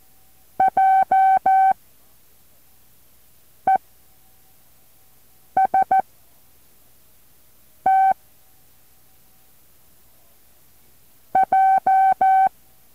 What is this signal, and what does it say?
Morse code, heard as a keyed tone, about 14 wpm, 770 Hz. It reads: J E S T J
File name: intl morse code2